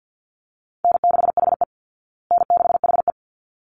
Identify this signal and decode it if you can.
Morse code, heard as a keyed tone, about 50 wpm, 700 Hz. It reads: N6HE N6HE
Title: Call Signs Repeated 3x - 50wpm